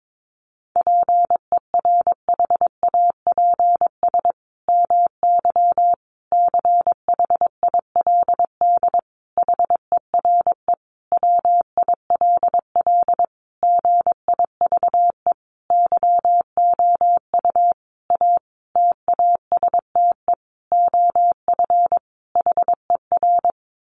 Morse code, heard as a keyed tone, about 22 wpm, 700 Hz. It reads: PERHAPS MY CHILD HERE WILL GIVE YOU A TASTE OF HER
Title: Sentences from Top 1100 Words 22wpm